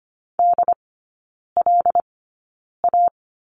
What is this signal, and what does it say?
Morse code, heard as a keyed tone, about 25 wpm, spaced as for 9 wpm, 700 Hz. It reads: DLA
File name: Lesson 17e - Letter u - Three Characters 25wpm